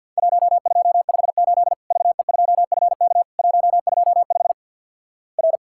Morse code, heard as a keyed tone, about 52 wpm, 700 Hz. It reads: 1258 VE2FK 125 R
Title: Worked All Europe DX Contest - QTC Exchange - 50wpm